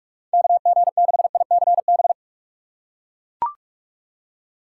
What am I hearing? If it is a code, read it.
Morse code, heard as a keyed tone, about 42 wpm, 700 Hz. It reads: KG6ICB
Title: USA Call Signs - 2x Character Spacing - 45wpm